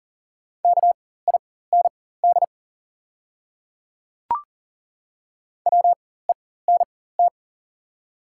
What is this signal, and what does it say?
Morse code, heard as a keyed tone, about 38 wpm, spaced as for 18 wpm, 700 Hz. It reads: KIND WENT
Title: 4x Character Spacing - Top 200 Words - 40wpm